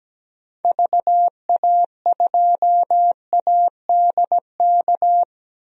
Morse code, heard as a keyed tone, about 17 wpm, 700 Hz. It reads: VA2ADK
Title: DXCC-100 Call Signs - 6 Characters - 17wpm